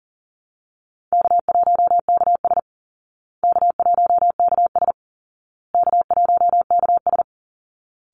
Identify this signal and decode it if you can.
Morse code, heard as a keyed tone, about 40 wpm, 700 Hz. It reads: K1KS K1KS K1KS
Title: Easy Call Signs Repeated 3x - 40wpm